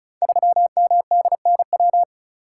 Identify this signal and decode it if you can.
Morse code, heard as a keyed tone, about 36 wpm, 700 Hz. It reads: 3MDNW
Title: Code Groups - 5 Letter:Numbers - 35wpm